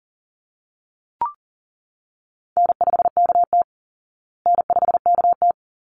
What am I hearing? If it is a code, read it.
Morse code, heard as a keyed tone, about 40 wpm, 700 Hz. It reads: N5KT N5KT
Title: Call Signs Repeated 3x - 40wpm